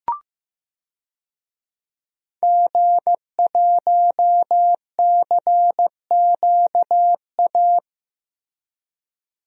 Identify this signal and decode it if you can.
Morse code, heard as a keyed tone, about 15 wpm, 700 Hz. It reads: G1CQA
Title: DXCC-100 Call Signs - 5 Characters - Repeated 3x - 15wpm